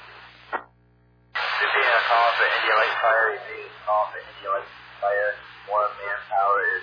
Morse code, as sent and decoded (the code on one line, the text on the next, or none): none